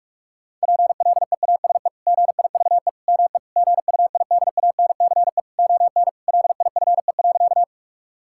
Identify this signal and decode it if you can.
Morse code, heard as a keyed tone, about 46 wpm, 700 Hz. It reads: PLEASE GIVE ME GUIDANCE ON LIFE.
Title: Sentences from Top 1600 Words 45wpm